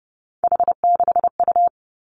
Morse code, heard as a keyed tone, about 30 wpm, 700 Hz. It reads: H6U
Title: Three Challenging Letter-Number Combinations 30wpm